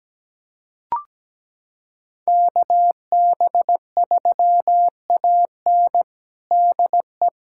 Morse code, heard as a keyed tone, about 17 wpm, 700 Hz. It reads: KB3AN DE
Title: USA Callsign DE Callsign - 5 Characters - 17wpm